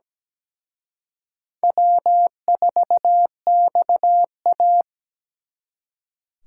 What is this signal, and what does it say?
Morse code, heard as a keyed tone, about 20 wpm, 700 Hz. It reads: W4XA
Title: Speed Racing - USA 1x2 Call Signs 17wpm